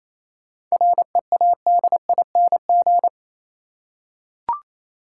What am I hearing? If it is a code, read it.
Morse code, heard as a keyed tone, about 27 wpm, 690 Hz. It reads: READING
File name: Easy Words - Speed-Racing with Hints - 22wpm